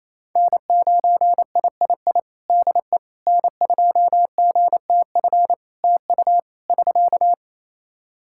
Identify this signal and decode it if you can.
Morse code, heard as a keyed tone, about 28 wpm, 700 Hz. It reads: N9III DE N2GTF TU <SK>
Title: Practice QSO - Exchange 4 - Ending - 28wpm